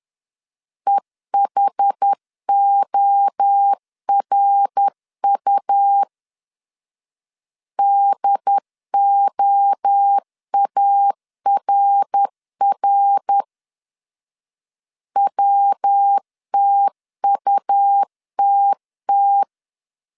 Morse code, heard as a keyed tone, about 10 wpm, 800 Hz. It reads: EHORU DOARR WTUTT
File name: CW / DET / 1 3 1500C S C 10wpm 3335